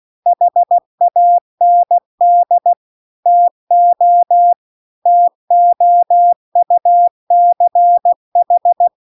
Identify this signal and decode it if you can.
Morse code, heard as a keyed tone, about 16 wpm, 700 Hz. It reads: HAND TO TOUCH
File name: CW-The-Big-Twoheaded-River-2-16WPM